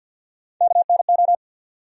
Morse code, heard as a keyed tone, about 51 wpm, 700 Hz. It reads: KNO
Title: USA Call Sign Postfixes - 3 Letters 50wpm